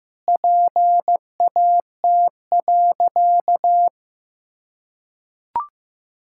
Morse code, heard as a keyed tone, about 15 wpm, 700 Hz. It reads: PAT.
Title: Pangram Challenge v2 - 15wpm